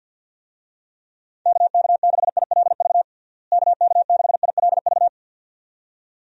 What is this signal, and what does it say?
Morse code, heard as a keyed tone, about 51 wpm, 700 Hz. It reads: KK6ILV KK6ILV
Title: USA Call Signs - 6 Characters - Repeated 2x - 50wpm